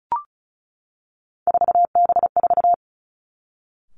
Morse code, heard as a keyed tone, about 36 wpm, 700 Hz. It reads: 4B4
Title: Three Challenging Letter-Number Combinations 35wpm